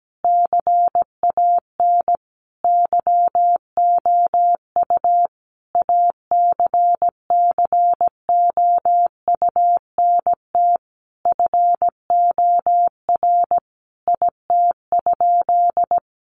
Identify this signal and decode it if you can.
Morse code, heard as a keyed tone, about 17 wpm, 700 Hz. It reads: CAN YOU ACCOUNT FOR IT?
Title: Sentences from Top 900 Words 17wpm